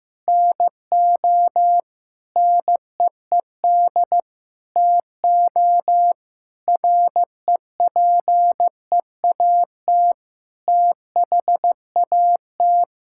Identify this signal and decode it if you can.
Morse code, heard as a keyed tone, about 16 wpm, 700 Hz. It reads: NO NEED TO REPEAT THAT
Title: Sentences from Top 1600 Words 15wpm